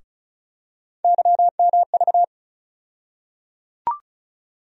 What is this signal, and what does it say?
Morse code, heard as a keyed tone, about 34 wpm, 700 Hz. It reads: YMV
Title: USA Call Sign Postfixes - 3 Letters 35wpm